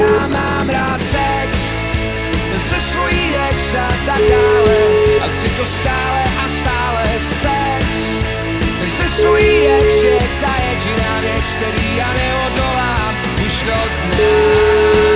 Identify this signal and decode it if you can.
Ringback tone (Europe, 425 Hz), with noise.